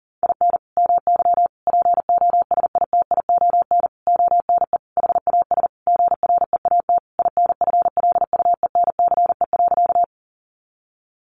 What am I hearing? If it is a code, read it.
Morse code, heard as a keyed tone, about 40 wpm, 700 Hz. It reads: IN MY POSITION ONE HAS GREAT INFLUENCE.
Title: Sentences from Top 1100 Words 40wpm